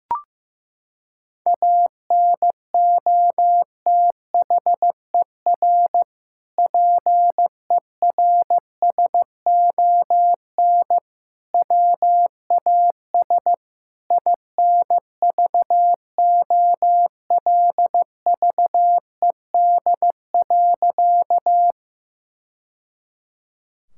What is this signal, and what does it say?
Morse code, heard as a keyed tone, about 15 wpm, 700 Hz. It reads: ANOTHER PERSON WAS INVOLVED.
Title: Sentences from Top 1000 Words - Review Part 5 - 15wpm